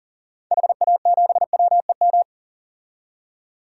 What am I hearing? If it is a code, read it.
Morse code, heard as a keyed tone, about 40 wpm, 700 Hz. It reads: HA7WEM
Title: DXCC-100 Call Signs - 6 Characters - 40wpm